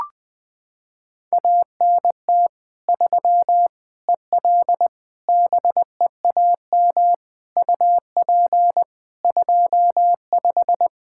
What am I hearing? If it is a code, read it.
Morse code, heard as a keyed tone, about 20 wpm, 700 Hz. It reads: ANT 3 EL BEAM UP 25
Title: K1USN SST QSO Practice 20wpm